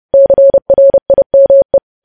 Morse code, heard as a keyed tone, about 30 wpm, 550 Hz. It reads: CRIME
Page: news headlines in morse code at 30 wpm